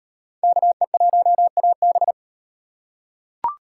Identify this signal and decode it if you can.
Morse code, heard as a keyed tone, about 38 wpm, 700 Hz. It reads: KE1AB